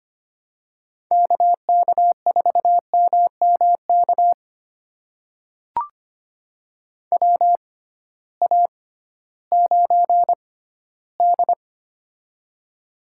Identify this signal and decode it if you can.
Morse code, heard as a keyed tone, about 25 wpm, spaced as for 15 wpm, 700 Hz. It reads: KK4MMK WA9D
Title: USA Call Signs - 6x Character Spacing - 25wpm